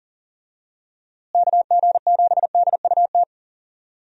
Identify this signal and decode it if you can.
Morse code, heard as a keyed tone, about 37 wpm, 700 Hz. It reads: KG7DUT
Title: USA Call Signs - 6 Characters - Repeated 3x - 40wpm